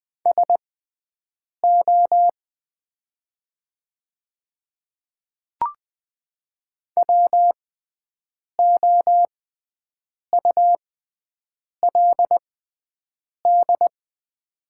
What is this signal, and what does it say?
Morse code, heard as a keyed tone, about 20 wpm, spaced as for 7 wpm, 700 Hz. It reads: SO WOULD
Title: 6x Character Spacing - Top 100 Words - 20wpm